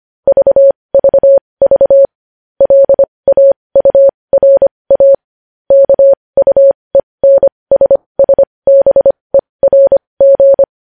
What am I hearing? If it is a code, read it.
Morse code, heard as a keyed tone, about 25 wpm, spaced as for 20 wpm, 550 Hz. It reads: VVV LAURA KUENSSBERG
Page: News Headlines in Morse Code at 20 WPM